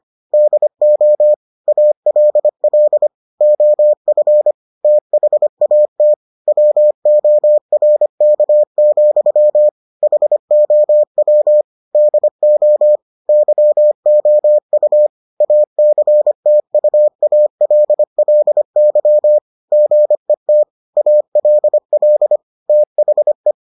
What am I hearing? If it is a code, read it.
Morse code, heard as a keyed tone, about 25 wpm, 600 Hz. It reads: DO ALL OF THAT WORK, HOW DO YOU ACTUALLY GET ALL THE